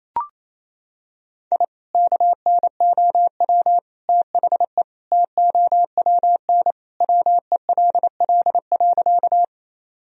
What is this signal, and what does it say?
Morse code, heard as a keyed tone, about 28 wpm, 700 Hz. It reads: I KNOW THE TOWN WELL.
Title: Sentences from Top 1000 Words - Review Part 2 - 28wpm